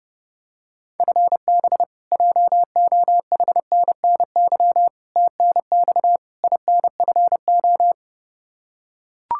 Morse code, heard as a keyed tone, about 32 wpm, 700 Hz. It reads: FB JOHNNY TNX INFO
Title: Practice QSO - Exchange 3 - More Info - 30wpm